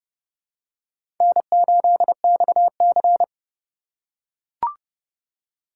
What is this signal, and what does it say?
Morse code, heard as a keyed tone, about 30 wpm, 700 Hz. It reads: N8XC
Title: USA Call Signs - 4x Character Spacing - 30wpm